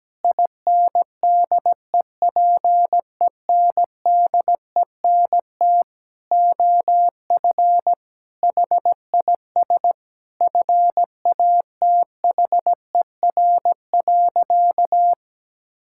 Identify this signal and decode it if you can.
Morse code, heard as a keyed tone, about 17 wpm, 700 Hz. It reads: INDEPENDENT OF HIS FATHER.
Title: Sentences from Top 1000 Words - Review Part 6 - 17wpm